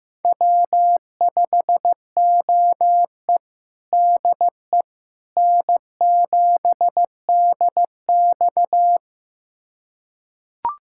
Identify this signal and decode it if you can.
Morse code, heard as a keyed tone, about 16 wpm, 700 Hz. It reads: W5OE DE N7DX
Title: USA Callsign DE Callsign - 4 Characters - 15wpm